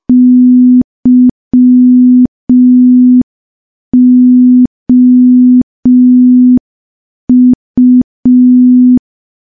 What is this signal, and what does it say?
Morse code, heard as a keyed tone, about 5 wpm, 250 Hz. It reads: YOU
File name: morse / morse-training-05WPM